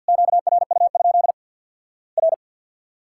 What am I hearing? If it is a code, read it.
Morse code, heard as a keyed tone, about 54 wpm, 700 Hz. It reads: QRU? R
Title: Worked All Europe DX Contest - QTC Exchange with Repeats - 50wpm